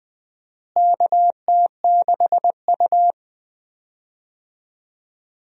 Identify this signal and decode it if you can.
Morse code, heard as a keyed tone, about 20 wpm, 700 Hz. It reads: KT6U
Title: USA Call Signs - 4 Characters - Repeated 2x - 20wpm